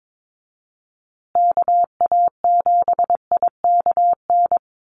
Morse code, heard as a keyed tone, about 22 wpm, 700 Hz. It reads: KA7IKN
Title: USA Call Signs Repeated 2x 22wpm